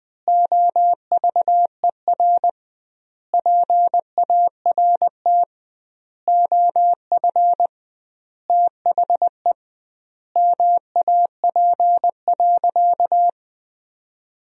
Morse code, heard as a keyed tone, about 21 wpm, 700 Hz. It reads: OVER PART OF THE MAP.
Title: 2x Word-Spacing - Sentences from Top 400 Words - 20wpm